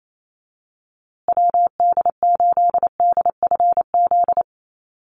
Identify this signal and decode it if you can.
Morse code, heard as a keyed tone, about 28 wpm, 700 Hz. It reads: WD8DFZ
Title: USA Call Signs Repeated 2x 28wpm